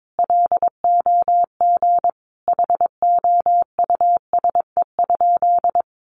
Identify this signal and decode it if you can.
Morse code, heard as a keyed tone, about 22 wpm, 700 Hz. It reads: LOG HOUSE?
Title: Sentences from Top 1400 Words 22wpm